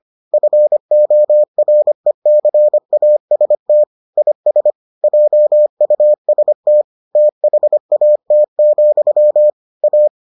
Morse code, heard as a keyed tone, about 25 wpm, 600 Hz. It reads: FORECAST IS JUST THAT, A